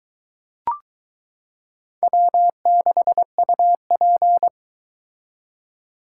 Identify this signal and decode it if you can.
Morse code, heard as a keyed tone, about 22 wpm, 700 Hz. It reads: W6UP